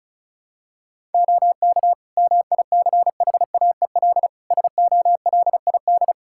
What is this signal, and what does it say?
Morse code, heard as a keyed tone, about 34 wpm, 700 Hz. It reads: OK MICHAEL SOLID
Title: K1USN SST QSO Practice 35wpm